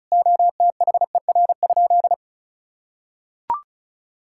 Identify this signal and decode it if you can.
Morse code, heard as a keyed tone, about 34 wpm, 700 Hz. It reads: OTHER?
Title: Sentences from Top 1600 Words 35wpm